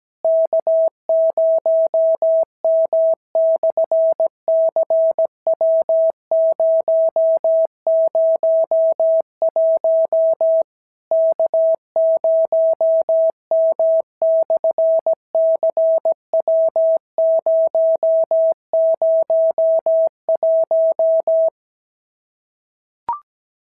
Morse code, heard as a keyed tone, about 17 wpm, 650 Hz. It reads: K0M/CW001 K0M/CW001
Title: SOTA QSO Practice 17wpm